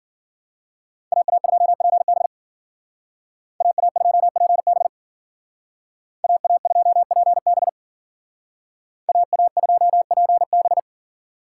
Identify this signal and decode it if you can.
Morse code, heard as a keyed tone, about 52 wpm, 700 Hz. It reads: AA2PB AA2PB AA2PB AA2PB